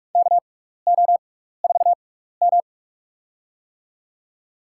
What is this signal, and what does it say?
Morse code, heard as a keyed tone, about 47 wpm, spaced as for 16 wpm, 700 Hz. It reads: KO4M
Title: USA Call Signs - 4 Characters - 6x Character Spacing - 45wpm